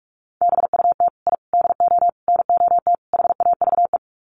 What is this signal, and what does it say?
Morse code, heard as a keyed tone, about 44 wpm, 700 Hz. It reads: BUT I DO NOT HAVE
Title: Sets of 5 Words Encore 45wpm